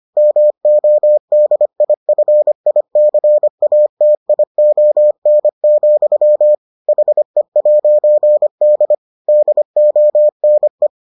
Morse code, heard as a keyed tone, about 25 wpm, 600 Hz. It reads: MODIFICATION, HE'D DONE